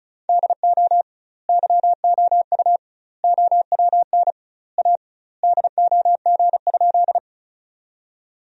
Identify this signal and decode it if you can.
Morse code, heard as a keyed tone, about 36 wpm, 700 Hz. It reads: DO YOU OWN A DOG?
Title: 2x Word-Spacing - Sentences from Top 400 Words - 35wpm